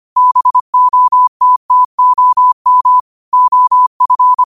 Morse code, heard as a keyed tone, about 25 wpm, 1000 Hz. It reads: BOTTOM OF
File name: qotdfiles / 25wpm